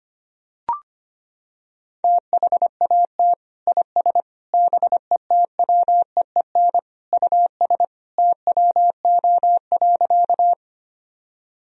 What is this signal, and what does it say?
Morse code, heard as a keyed tone, about 25 wpm, 700 Hz. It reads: THAT IS BETWEEN US TWO.
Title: Sentences from Top 1000 Words - Review Part 1 - 25wpm